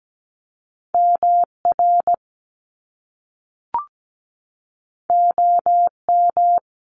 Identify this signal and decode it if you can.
Morse code, heard as a keyed tone, about 17 wpm, 700 Hz. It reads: MR OM
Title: Top Two Letter Combinations in Words 17wpm